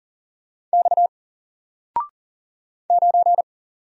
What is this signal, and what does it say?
Morse code, heard as a keyed tone, about 40 wpm, 700 Hz. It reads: X 9
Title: Single Character (Letter:Number:Punctuation:Prosign) Alt 40wpm